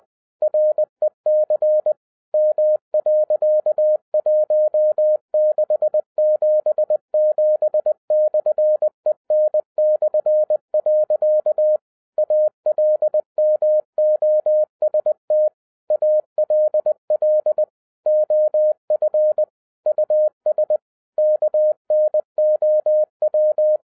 Morse code, heard as a keyed tone, about 20 wpm, 610 Hz. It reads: REC M.1677/EN/. ALMOST ALL OF US KNOW